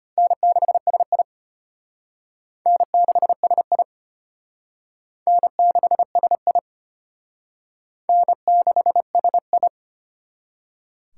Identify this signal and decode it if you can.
Morse code, heard as a keyed tone, about 31 wpm, 700 Hz. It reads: N6SI N6SI N6SI N6SI
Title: Speed Racing - USA 1x2 Call Signs 25wpm